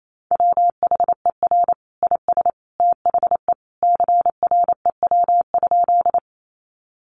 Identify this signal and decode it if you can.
Morse code, heard as a keyed tone, about 28 wpm, 700 Hz. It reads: WHER IS THE CREW?
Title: Sentences from Top 1500 Words 28wpm